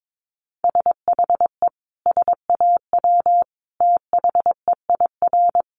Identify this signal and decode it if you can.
Morse code, heard as a keyed tone, about 22 wpm, 700 Hz. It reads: SHE SAW THEIR
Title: Sentences from Top 1200 Words 22wpm